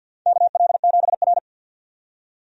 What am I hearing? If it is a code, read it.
Morse code, heard as a keyed tone, about 53 wpm, 700 Hz. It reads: KL7R
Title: USA Call Signs - 4 Characters - 50wpm